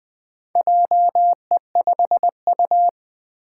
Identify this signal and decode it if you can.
Morse code, heard as a keyed tone, about 20 wpm, 700 Hz. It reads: JE5U
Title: Four Challenging Letter-Number Combinations 20wpm